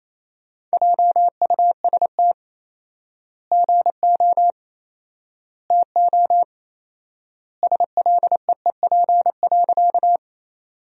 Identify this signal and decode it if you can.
Morse code, heard as a keyed tone, about 28 wpm, 700 Hz. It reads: JUST GO TO SLEEP.
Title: 4x Word-Spacing - Sentences from Top 500 Words - 28wpm